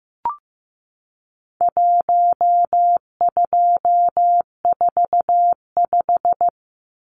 Morse code, heard as a keyed tone, about 15 wpm, 700 Hz. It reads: 1245
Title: Numbers - Contest Style - 4-Digit 15wpm